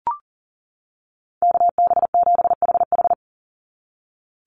Morse code, heard as a keyed tone, about 40 wpm, 700 Hz. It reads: KB7HH
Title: NA SOTA Chaser Callsigns 40wpm